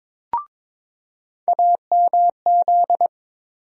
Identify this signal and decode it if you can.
Morse code, heard as a keyed tone, about 22 wpm, 700 Hz. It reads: AMZ
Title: USA Call Sign Postfixes - 3 Letters 22wpm